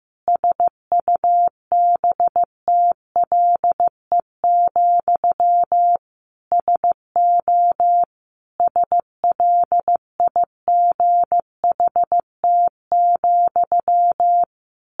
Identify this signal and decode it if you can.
Morse code, heard as a keyed tone, about 15 wpm, 700 Hz. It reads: SUBTLE, SO SLIGHT,
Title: A Sound Of Thunder 15wpm